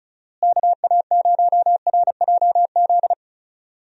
Morse code, heard as a keyed tone, about 36 wpm, 700 Hz. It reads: KA0RJZ
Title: USA Call Signs - 6 Characters - Repeated 3x - 35wpm